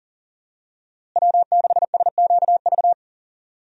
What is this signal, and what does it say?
Morse code, heard as a keyed tone, about 40 wpm, 700 Hz. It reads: W6SQV
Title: USA Call Signs - 5 Characters - Repeated 3x - 40wpm